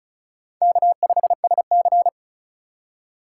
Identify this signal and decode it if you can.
Morse code, heard as a keyed tone, about 36 wpm, 700 Hz. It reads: K5SC
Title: USA Call Signs - 2x Character Spacing - 35wpm